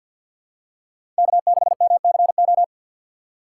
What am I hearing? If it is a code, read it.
Morse code, heard as a keyed tone, about 48 wpm, 700 Hz. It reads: K6MCO
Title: USA Call Signs - 5 Characters - 3x Character Spacing - 50wpm